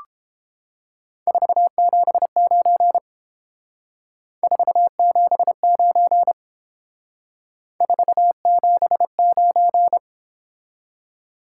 Morse code, heard as a keyed tone, about 29 wpm, 700 Hz. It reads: 479 479 479